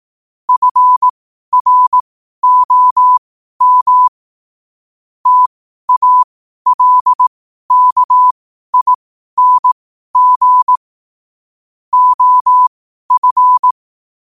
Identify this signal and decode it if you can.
Morse code, heard as a keyed tone, about 18 wpm, spaced as for 12 wpm, 1000 Hz. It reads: FROM TALKING OF
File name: 12wpm